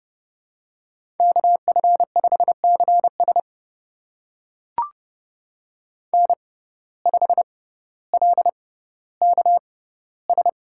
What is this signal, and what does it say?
Morse code, heard as a keyed tone, about 30 wpm, spaced as for 15 wpm, 700 Hz. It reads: KF5CS N5LKS
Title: USA Call Signs - 5 Characters - 6x Character Spacing - 30wpm